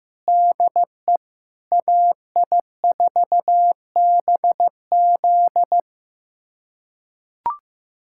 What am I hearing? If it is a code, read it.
Morse code, heard as a keyed tone, about 16 wpm, 700 Hz. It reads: DE AI4BZ
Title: USA Callsign DE Callsign 15wpm